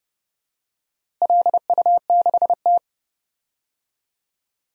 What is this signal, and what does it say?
Morse code, heard as a keyed tone, about 29 wpm, 700 Hz. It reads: LU6T
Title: DXCC-100 Call Signs - 4 Characters - Repeated 2x - 30wpm